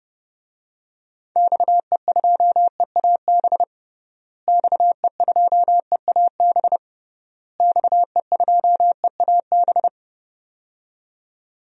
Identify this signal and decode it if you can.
Morse code, heard as a keyed tone, about 30 wpm, 700 Hz. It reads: XE2EAB XE2EAB XE2EAB
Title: DXCC-100 Call Signs - 6 Characters - Repeated 3x - 30wpm